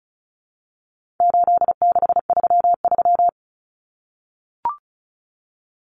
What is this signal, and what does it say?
Morse code, heard as a keyed tone, about 34 wpm, 700 Hz. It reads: 8633
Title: Numbers - 4-Digit 35wpm